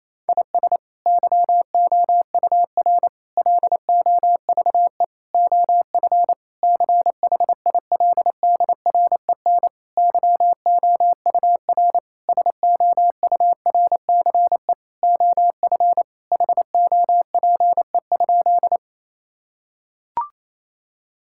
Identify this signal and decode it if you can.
Morse code, heard as a keyed tone, about 28 wpm, 700 Hz. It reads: IS YOUR LOVE OF CHILDREN YOUR SOURCE OF HOPE?
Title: Sentences from Top 1000 Words - Review Part 5 - 28wpm